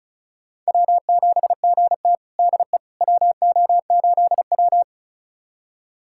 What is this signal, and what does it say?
Morse code, heard as a keyed tone, about 36 wpm, 700 Hz. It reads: W7GT DE WO8W
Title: USA Callsign DE Callsign - 4 Characters - 35wpm